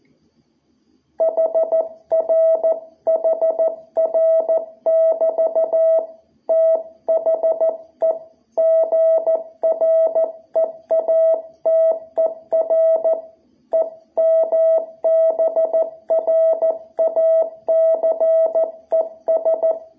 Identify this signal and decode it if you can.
Morse code, heard as a keyed tone, about 14 wpm, 660 Hz. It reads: HR HR <BT> THE GREATER EMBRACES
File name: jh8cbh / 2021cw